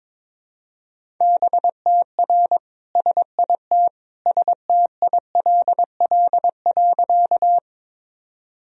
Morse code, heard as a keyed tone, about 21 wpm, 700 Hz. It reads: BTR SIT STILL.
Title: Sentences from Top 500 Words and Abbreviations 22wpm